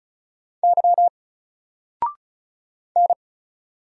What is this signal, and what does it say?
Morse code, heard as a keyed tone, about 35 wpm, 700 Hz. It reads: Y N
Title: Single Letter-Number Alt 35wpm